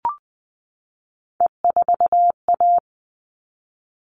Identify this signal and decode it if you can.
Morse code, heard as a keyed tone, about 20 wpm, 700 Hz. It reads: E4A
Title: All Cut Numbers - 3-Digit 20wpm